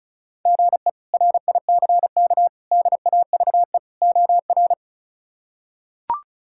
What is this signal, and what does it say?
Morse code, heard as a keyed tone, about 36 wpm, 700 Hz. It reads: GE RICK DAVE OR
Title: K1USN SST QSO Practice - 35wpm